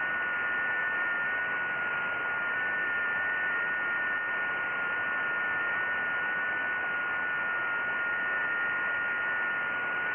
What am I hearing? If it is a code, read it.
FT8.
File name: IC-7100